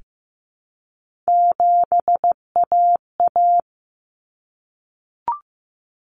Morse code, heard as a keyed tone, about 15 wpm, 700 Hz. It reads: 7AA
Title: Common Cut Numbers - 3-Digit 15wpm